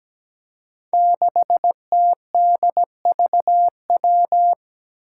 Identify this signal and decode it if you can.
Morse code, heard as a keyed tone, about 17 wpm, 700 Hz. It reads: 6TDVW
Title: Five Challenging Letter-Number Combinations 17wpm